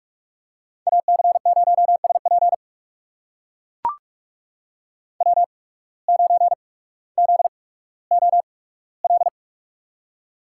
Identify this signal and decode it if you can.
Morse code, heard as a keyed tone, about 44 wpm, spaced as for 19 wpm, 700 Hz. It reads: AC0SP W9ZOL
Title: USA Call Signs - 5 Characters - 8x Character Spacing - 45wpm